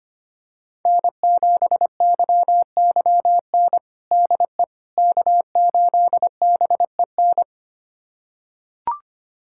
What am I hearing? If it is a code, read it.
Morse code, heard as a keyed tone, about 25 wpm, 700 Hz. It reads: N7YYN DE K8BEN
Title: USA Callsign DE Callsign - 5 Characters - 25wpm